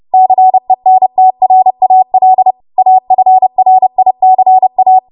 Morse code, heard as a keyed tone, about 30 wpm, 750 Hz. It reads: CENTRAL AFRICA
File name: Morse / 30WPM